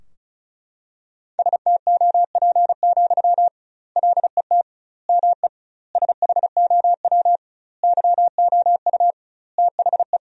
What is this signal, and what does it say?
Morse code, heard as a keyed tone, about 34 wpm, 700 Hz. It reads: STOP, LET ME SHOW YOU THE
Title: 2x Word-Spacing - Sentences from Top 300 Words - 35wpm